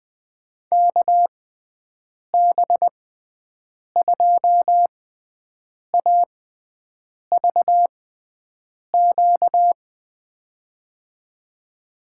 Morse code, heard as a keyed tone, about 20 wpm, spaced as for 7 wpm, 700 Hz. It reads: KB2AVQ
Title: USA Call Signs - 6 Characters - 6x Character Spacing - 20wpm